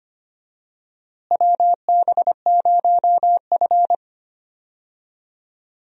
Morse code, heard as a keyed tone, about 25 wpm, 700 Hz. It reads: WB0F
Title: USA Call Signs - 4 Characters - Repeated 2x - 25wpm